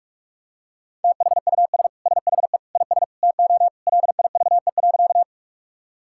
Morse code, heard as a keyed tone, about 46 wpm, 700 Hz. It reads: THUS SHE IS TO LIVE.
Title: Sentences from Top 1000 Words - Review Part 5 - 45wpm